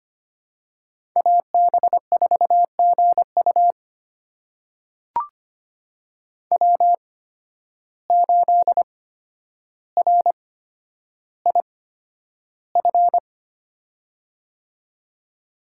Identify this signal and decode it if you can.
Morse code, heard as a keyed tone, about 25 wpm, spaced as for 11 wpm, 700 Hz. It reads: AB4GU W8RIF
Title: USA Call Signs - 5 Characters - 8x Character Spacing - 25wpm